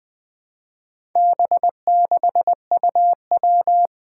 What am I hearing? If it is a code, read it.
Morse code, heard as a keyed tone, about 21 wpm, 700 Hz. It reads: B6UW
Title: Four Challenging Letter-Number Combinations 20wpm